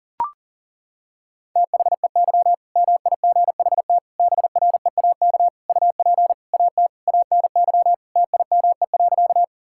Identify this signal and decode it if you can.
Morse code, heard as a keyed tone, about 40 wpm, 700 Hz. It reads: THEY MIGHT BREAK UP AT ANY TIME.
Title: Sentences from Top 1000 Words - Review Part 6 - 40wpm